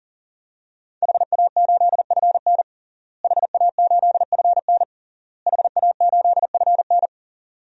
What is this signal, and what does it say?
Morse code, heard as a keyed tone, about 40 wpm, 700 Hz. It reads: HA8FN HA8FN HA8FN
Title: DXCC-100 Call Signs - 5 Characters - Repeated 3x - 40wpm